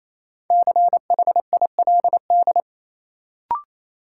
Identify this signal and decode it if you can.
Morse code, heard as a keyed tone, about 28 wpm, 700 Hz. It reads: CHILD
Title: Sets of 5 Words Encore - New Word Drill - 28wpm